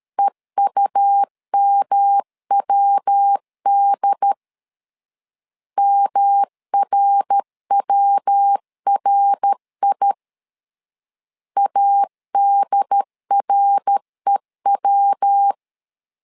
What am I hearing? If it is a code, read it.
Morse code, heard as a keyed tone, about 13 wpm, 800 Hz. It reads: EUMWD MRWRI ADREW
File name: LEZIONI / 3 MWRO / 3 MWR 12wpm 1711